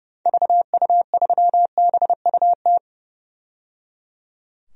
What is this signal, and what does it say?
Morse code, heard as a keyed tone, about 30 wpm, 700 Hz. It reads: VU3BUT
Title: DXCC-100 Call Signs - 6 Characters - 30wpm